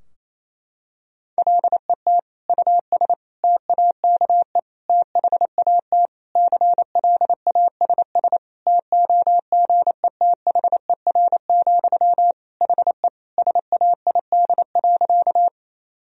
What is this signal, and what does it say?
Morse code, heard as a keyed tone, about 28 wpm, 700 Hz. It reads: LET US TAKE THAT CLASS TOGETHER, HE SAID.
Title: Sentences from Top 1000 Words - Review Part 2 - 28wpm